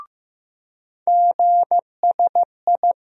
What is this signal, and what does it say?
Morse code, heard as a keyed tone, about 15 wpm, 700 Hz. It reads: GSI
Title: USA Call Sign Postfixes - 3 Letters 15wpm